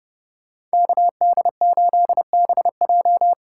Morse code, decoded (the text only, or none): KD8BJ